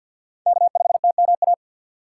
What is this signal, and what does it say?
Morse code, heard as a keyed tone, about 46 wpm, 700 Hz. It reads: K5TMA
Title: USA Call Signs - 5 Characters - 3x Character Spacing - 50wpm